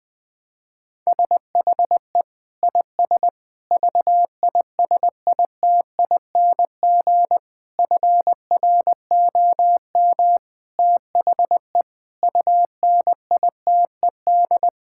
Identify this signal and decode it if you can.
Morse code, heard as a keyed tone, about 20 wpm, 700 Hz. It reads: SHE IS VISITING FROM THE UNITED
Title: Sentences with 4-Word Phrases - 20wpm